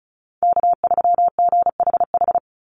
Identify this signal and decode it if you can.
Morse code, heard as a keyed tone, about 34 wpm, 700 Hz. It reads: K3GHH
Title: Call Signs - 35wpm